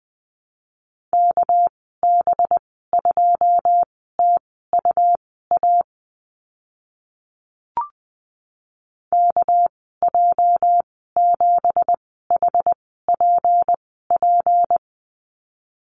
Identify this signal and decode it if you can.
Morse code, heard as a keyed tone, about 19 wpm, spaced as for 14 wpm, 700 Hz. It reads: KB2TUA KJ7HPP
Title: 207 Lesson 33g - Letter j - Call Signs 20wpm